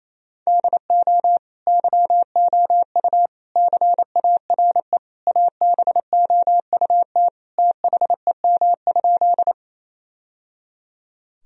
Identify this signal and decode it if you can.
Morse code, heard as a keyed tone, about 27 wpm, 700 Hz. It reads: DO YOU CARE ABOUT THEM?
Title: Sentences from Top 1000 Words - Review Part 1 - 28wpm